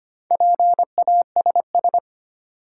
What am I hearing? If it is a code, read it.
Morse code, heard as a keyed tone, about 25 wpm, 700 Hz. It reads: PASS
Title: Sentences from Top 400 Words - New Word Drill - 25wpm